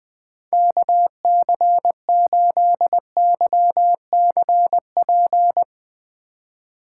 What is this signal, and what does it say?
Morse code, heard as a keyed tone, about 21 wpm, 700 Hz. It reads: KC8YCP
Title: USA Call Signs - 6 Characters - 3x Character Spacing - 20wpm